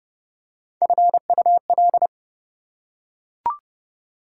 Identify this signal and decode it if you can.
Morse code, heard as a keyed tone, about 29 wpm, 700 Hz. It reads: FUL
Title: Prefixes and Suffixes 30wpm